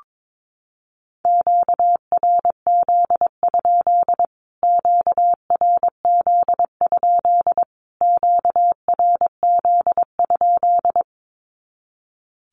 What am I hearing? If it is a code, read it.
Morse code, heard as a keyed tone, about 22 wpm, 700 Hz. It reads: QRZ? QRZ? QRZ?
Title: Q-Codes Repeated 3x 22wpm